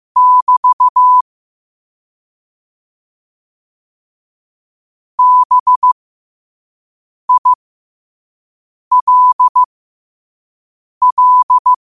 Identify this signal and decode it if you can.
Morse code, heard as a keyed tone, about 15 wpm, spaced as for 5 wpm, 1000 Hz. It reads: <BT> BILL